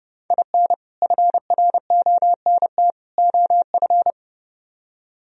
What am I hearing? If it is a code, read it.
Morse code, heard as a keyed tone, about 31 wpm, 700 Hz. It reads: IN FRONT OF